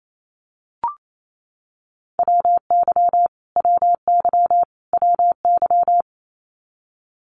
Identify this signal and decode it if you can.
Morse code, heard as a keyed tone, about 28 wpm, 700 Hz. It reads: WY WY WY
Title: US States Repeated 3x 28wpm